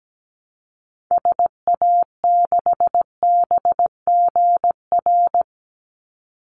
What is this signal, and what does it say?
Morse code, heard as a keyed tone, about 17 wpm, 700 Hz. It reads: SA6BGR
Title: Call Signs - 17wpm